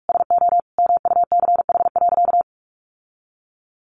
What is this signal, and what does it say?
Morse code, heard as a keyed tone, about 44 wpm, 700 Hz. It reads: SO MUCH.
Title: Sentences from Top 1400 Words 45wpm